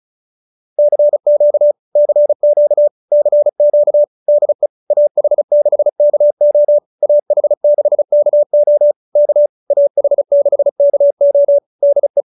Morse code, heard as a keyed tone, about 34 wpm, 590 Hz. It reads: CQ CQ CQ DE AH6KO AH6KO K AH6KO DE